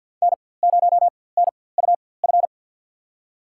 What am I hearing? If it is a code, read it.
Morse code, heard as a keyed tone, about 51 wpm, spaced as for 23 wpm, 700 Hz. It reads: N0NUF
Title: USA Call Signs - 5 Characters - 4x Character Spacing - 50wpm